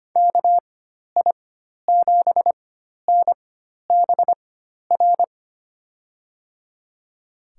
Morse code, heard as a keyed tone, about 25 wpm, spaced as for 12 wpm, 700 Hz. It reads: KI7NBR
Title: USA Call Signs - 4x Character Spacing - 25wpm